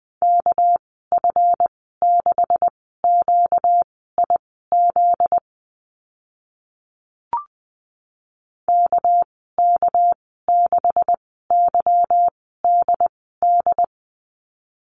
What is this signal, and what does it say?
Morse code, heard as a keyed tone, about 20 wpm, spaced as for 14 wpm, 700 Hz. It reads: KF6QIZ KK6YDD
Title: 245 Lesson 39g - Letter z - Call Signs 20wpm